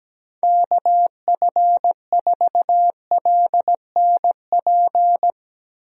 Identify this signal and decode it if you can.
Morse code, heard as a keyed tone, about 17 wpm, 700 Hz. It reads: KF4LNP
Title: USA Call Signs - 6 Characters - Repeated 3x - 17wpm